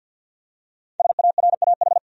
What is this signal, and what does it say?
Morse code, heard as a keyed tone, about 52 wpm, 700 Hz. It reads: SARAH
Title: USA Common Names - Max 5 Letters - 50wpm